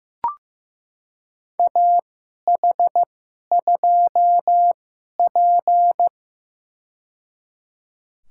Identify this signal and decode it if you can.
Morse code, heard as a keyed tone, about 15 wpm, spaced as for 11 wpm, 700 Hz. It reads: AH2P
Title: USA Call Signs - 4 Characters - 2x Character Spacing - 15wpm